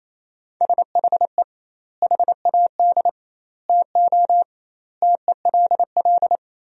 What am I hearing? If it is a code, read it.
Morse code, heard as a keyed tone, about 28 wpm, 700 Hz. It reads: SHE HAD TO TELL
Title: 2x Word-Spacing - Sentences from Top 200 Words - 28wpm